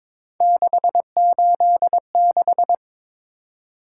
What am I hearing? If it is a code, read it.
Morse code, heard as a keyed tone, about 22 wpm, 700 Hz. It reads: 686
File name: practice / 22wpm / Three Letter-Number Combinations - 22wpm